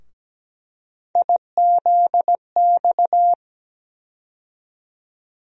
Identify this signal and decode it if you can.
Morse code, heard as a keyed tone, about 17 wpm, 700 Hz. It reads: IZX
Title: USA Call Sign Postfixes - 3 Letters 17wpm